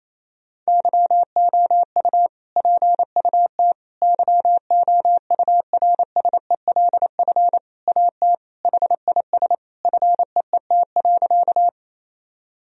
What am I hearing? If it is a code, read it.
Morse code, heard as a keyed tone, about 28 wpm, 700 Hz. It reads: YOU PUT YOURSELF AT HIS FEET.
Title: Sentences from Top 1000 Words - Review Part 1 - 28wpm